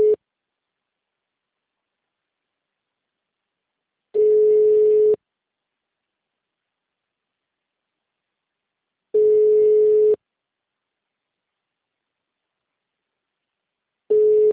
Ringback tone (Europe, 425 Hz).
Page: Capital - SP